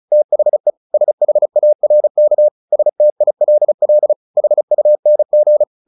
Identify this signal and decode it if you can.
Morse code, heard as a keyed tone, about 34 wpm, 600 Hz. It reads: THE SHARK STILL HUNG
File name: the old man and the sea 2 35wpm